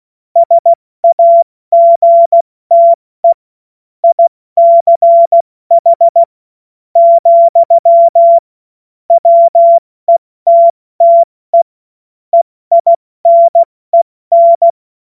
Morse code, heard as a keyed tone, about 16 wpm, spaced as for 14 wpm, 680 Hz. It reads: SAGTE ICH, WETTE EINEN